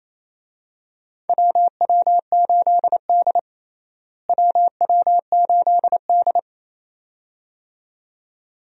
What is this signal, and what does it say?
Morse code, heard as a keyed tone, about 28 wpm, 700 Hz. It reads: WW8D WW8D
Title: USA Call Signs - 4 Characters - Repeated 2x - 28wpm